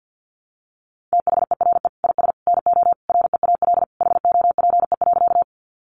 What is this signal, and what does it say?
Morse code, heard as a keyed tone, about 50 wpm, 700 Hz. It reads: THERE IS NO REAL HOPE.
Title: Sentences from Top 1000 Words 50wpm